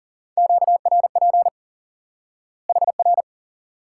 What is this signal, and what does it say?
Morse code, heard as a keyed tone, about 40 wpm, 700 Hz. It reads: QRP HR
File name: IQE Course 3 - Lesson 89 - YAGI - Two Elements - 40wpm